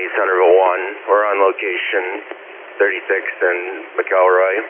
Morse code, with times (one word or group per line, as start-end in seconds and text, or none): none